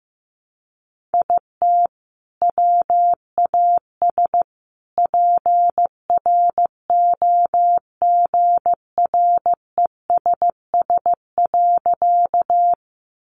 Morse code, heard as a keyed tone, about 15 wpm, 700 Hz. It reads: IT WAS PROGRESS.
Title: Sentences from Top 1400 Words 15wpm